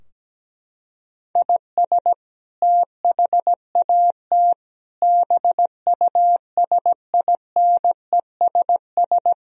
Morse code, heard as a keyed tone, about 17 wpm, 700 Hz. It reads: IS THAT BUSINESS
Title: Sentences from Top 1000 Words - Review Part 3 - 17wpm